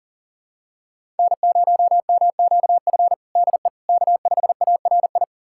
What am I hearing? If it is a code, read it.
Morse code, heard as a keyed tone, about 40 wpm, 700 Hz. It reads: N0MQF DE K5ARI
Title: USA Callsign DE Callsign 40wpm